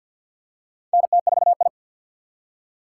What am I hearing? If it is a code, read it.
Morse code, heard as a keyed tone, about 53 wpm, 700 Hz. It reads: NT4I
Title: USA Call Signs - 4 Characters - 3x Character Spacing - 50wpm